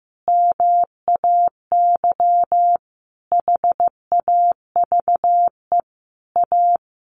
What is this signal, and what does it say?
Morse code, heard as a keyed tone, about 15 wpm, 700 Hz. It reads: MAY HAVE A
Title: Sentences from Top 1400 Words 15wpm